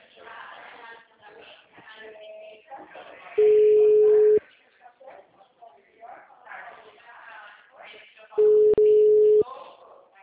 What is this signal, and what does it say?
Ringback tone (Europe, 425 Hz), with speech.
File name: out- -2014- - - .19123